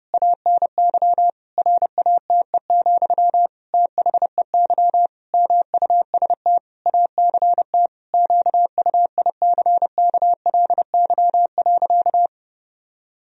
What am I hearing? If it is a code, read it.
Morse code, heard as a keyed tone, about 32 wpm, 700 Hz. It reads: ANY RATE, THEY MUST ACT QUICKLY.
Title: Sentences from Top 1000 Words - Review Part 4 - 30wpm